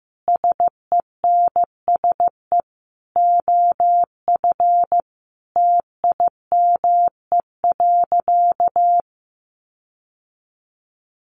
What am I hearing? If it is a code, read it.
Morse code, heard as a keyed tone, about 15 wpm, 700 Hz. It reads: SENSE OF TIME.
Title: Sentences from Top 1400 Words 15wpm